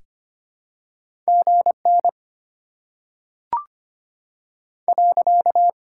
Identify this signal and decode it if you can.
Morse code, heard as a keyed tone, about 25 wpm, 700 Hz. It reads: G N .
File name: Top 90 QSO Elements - 6x Character Spacing - 25wpm